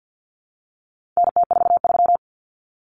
Morse code, heard as a keyed tone, about 50 wpm, 700 Hz. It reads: NT43
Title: Common Cut Numbers - 4-Digit 50wpm